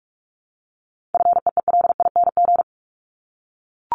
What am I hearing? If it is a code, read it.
Morse code, heard as a keyed tone, about 44 wpm, 700 Hz. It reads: FEELING
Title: Top 1001-1250 Words 45wpm